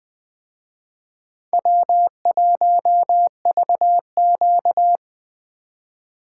Morse code, heard as a keyed tone, about 20 wpm, 700 Hz. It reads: W1VQ
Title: Speed Racing - USA Call Signs 17wpm